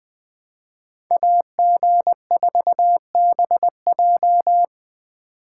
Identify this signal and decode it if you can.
Morse code, heard as a keyed tone, about 24 wpm, 700 Hz. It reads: AG4BJ
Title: Speed Racing - USA 2x2 Call Signs 20wpm